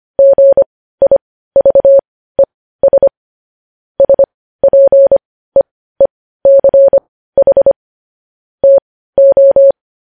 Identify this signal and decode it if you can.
Morse code, heard as a keyed tone, about 25 wpm, spaced as for 15 wpm, 550 Hz. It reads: GIVES SPEECH TO
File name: podcast / News headlines-1x-25x15